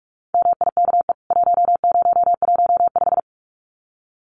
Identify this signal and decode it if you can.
Morse code, heard as a keyed tone, about 44 wpm, 700 Hz. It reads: MIKE 1015
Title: CWT Exchange 45wpm